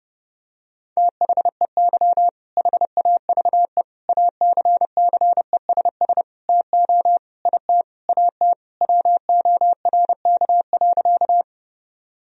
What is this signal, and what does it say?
Morse code, heard as a keyed tone, about 30 wpm, 700 Hz. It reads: THEY HAVE ACCESS TO IT AT WORK.
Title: Sentences from Top 1000 Words - Review Part 4 - 30wpm